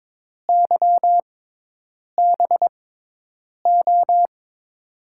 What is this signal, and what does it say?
Morse code, heard as a keyed tone, about 22 wpm, spaced as for 8 wpm, 700 Hz. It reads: YBO
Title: USA Call Signs - 5 Characters - 6x Character Spacing - 22wpm